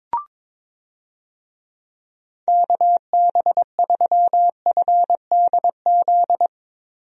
Morse code, heard as a keyed tone, about 22 wpm, 700 Hz. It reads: KB3FDZ
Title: USA Call Signs - 6 Characters - 22wpm